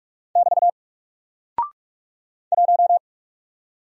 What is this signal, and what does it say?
Morse code, heard as a keyed tone, about 44 wpm, 700 Hz. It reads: <BT> 1
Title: Single Character (Letter:Number:Punctuation:Prosign) Alt 45wpm